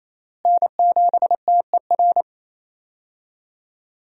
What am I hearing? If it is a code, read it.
Morse code, heard as a keyed tone, about 27 wpm, 700 Hz. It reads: N7TER
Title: USA Call Signs - 5 Characters - Repeated 3x - 28wpm